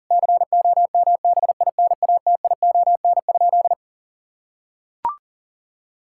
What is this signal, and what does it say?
Morse code, heard as a keyed tone, about 40 wpm, 700 Hz. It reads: COMBINATION?
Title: Sentences from Top 1000 Words - Review Part 6 - 40wpm